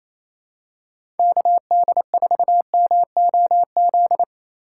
Morse code, heard as a keyed tone, about 28 wpm, 700 Hz. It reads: KD4MOZ
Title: USA Call Signs - 6 Characters - 3x Character Spacing - 28wpm